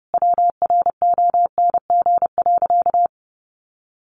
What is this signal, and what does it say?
Morse code, heard as a keyed tone, about 30 wpm, 700 Hz. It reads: WRONG.
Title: Sentences from Top 900 Words 30wpm